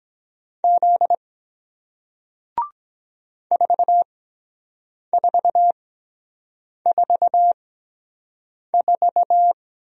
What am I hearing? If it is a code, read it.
Morse code, heard as a keyed tone, about 23 wpm, spaced as for 7 wpm, 700 Hz. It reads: Z 4444